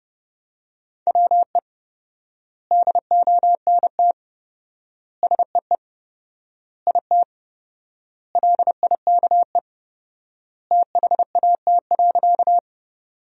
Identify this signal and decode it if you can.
Morse code, heard as a keyed tone, about 30 wpm, 700 Hz. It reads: WE DONT SEE IT LIKE THAT.
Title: 4x Word-Spacing - Sentences from Top 300 Words - 30wpm